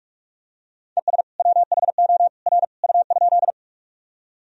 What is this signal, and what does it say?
Morse code, heard as a keyed tone, about 46 wpm, 700 Hz. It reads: ES WHO R U?
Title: Sentences from Top 100 Words and Abbreviations 45wpm